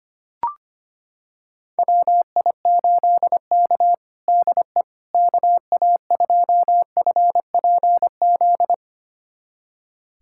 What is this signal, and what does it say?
Morse code, heard as a keyed tone, about 25 wpm, 700 Hz. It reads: WI8K DE KA2FPZ
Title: USA Callsign DE Callsign 25wpm